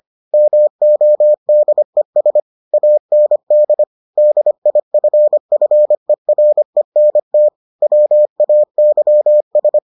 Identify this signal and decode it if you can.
Morse code, heard as a keyed tone, about 25 wpm, 600 Hz. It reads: MODES AND DIFFERENT WAYS